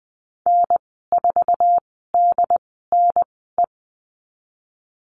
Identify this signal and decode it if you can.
Morse code, heard as a keyed tone, about 20 wpm, spaced as for 15 wpm, 700 Hz. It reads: N4DNE
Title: 069 Lesson 12g - Letter d - Call Signs 20wpm